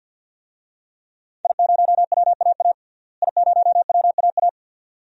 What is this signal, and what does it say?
Morse code, heard as a keyed tone, about 47 wpm, 700 Hz. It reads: I0WAA I0WAA
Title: DXCC-100 Call Signs - 5 Characters - Repeated 2x - 50wpm